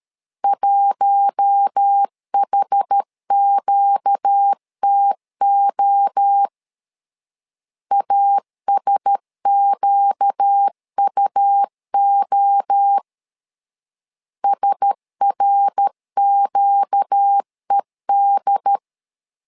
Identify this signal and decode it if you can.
Morse code, heard as a keyed tone, about 13 wpm, 800 Hz. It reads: 1HQTO ASQUO SRQED
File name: LEZIONI / 4 Q1S 12wpm 1802